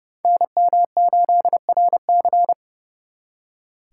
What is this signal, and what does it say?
Morse code, heard as a keyed tone, about 39 wpm, 700 Hz. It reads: NM8RC